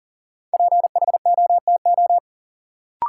Morse code, heard as a keyed tone, about 37 wpm, 700 Hz. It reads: PHOTO
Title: Sentences from Top 1200 Words - New Word Drill - 40wpm